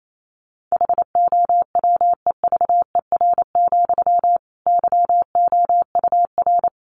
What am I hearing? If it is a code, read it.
Morse code, heard as a keyed tone, about 28 wpm, 700 Hz. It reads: HOWEVER, YOUR
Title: Sentences from Top 1200 Words 28wpm